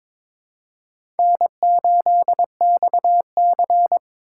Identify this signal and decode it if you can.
Morse code, heard as a keyed tone, about 22 wpm, 700 Hz. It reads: N8XC
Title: USA Call Signs - 4 Characters - 6x Character Spacing - 22wpm